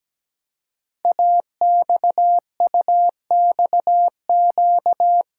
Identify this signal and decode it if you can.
Morse code, heard as a keyed tone, about 17 wpm, 700 Hz. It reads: AXUXQ
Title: Code Groups - 5 Letters - 17wpm